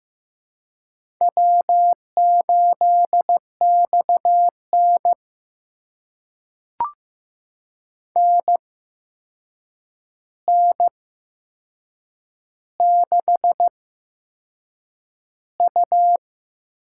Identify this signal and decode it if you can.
Morse code, heard as a keyed tone, about 15 wpm, spaced as for 6 wpm, 700 Hz. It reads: W8XN NN6U
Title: USA Call Signs - 4 Characters - 8x Character Spacing - 15wpm